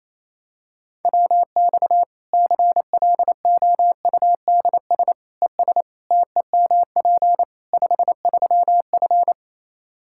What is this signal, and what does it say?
Morse code, heard as a keyed tone, about 27 wpm, 700 Hz. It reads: WX CLOUDS ES TEMP 53F
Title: Practice QSO - Exchange 2 - Info - 28wpm